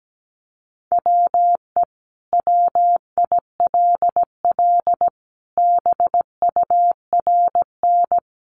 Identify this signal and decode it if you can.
Morse code, heard as a keyed tone, about 17 wpm, 700 Hz. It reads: WE WILL BURN
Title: Sentences from Top 1300 Words 17wpm